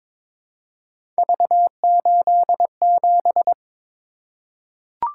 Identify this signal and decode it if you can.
Morse code, heard as a keyed tone, about 22 wpm, 700 Hz. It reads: V87
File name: practice / 22wpm / Three Letter-Number Combinations - 22wpm